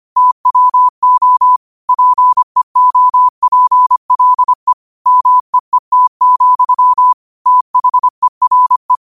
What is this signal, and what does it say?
Morse code, heard as a keyed tone, about 25 wpm, 1000 Hz. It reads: TWO PEOPLE MEET, THERE